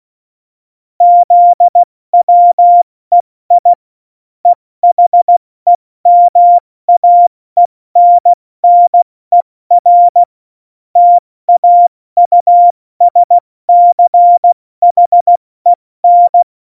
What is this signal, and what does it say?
Morse code, heard as a keyed tone, about 16 wpm, 700 Hz. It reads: ZWEI EHEMAENNER TAUSCHEN
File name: Morsekurs / Witze / Witz 23. Juli 2019, 70 F80